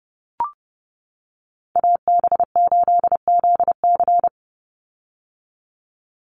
Morse code, heard as a keyed tone, about 30 wpm, 700 Hz. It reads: AB8ZC
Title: USA Call Signs 30wpm